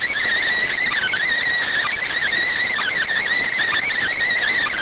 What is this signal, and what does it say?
Emergency Alert System (SAME) tones, heard as harsh buzzing bursts. It is heard from 0.5 to 4.6 s.